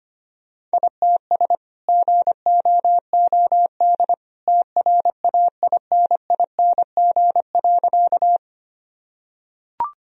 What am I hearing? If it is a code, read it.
Morse code, heard as a keyed tone, about 25 wpm, 700 Hz. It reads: ITS GOOD TRAINING.
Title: Sentences from Top 1000 Words - Review Part 4 - 25wpm